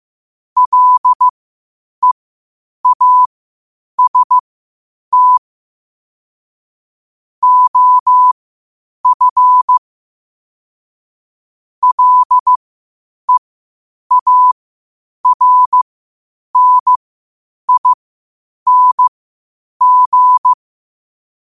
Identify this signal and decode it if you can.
Morse code, heard as a keyed tone, about 15 wpm, spaced as for 8 wpm, 1000 Hz. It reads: LEAST OF LEARNING
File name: qotdfiles / 8wpm